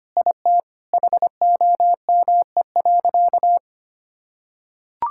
Morse code, heard as a keyed tone, about 25 wpm, 700 Hz. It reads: IT HOME.
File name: practice / 25wpm / Sentences with 4-Word Phrases - 25wpm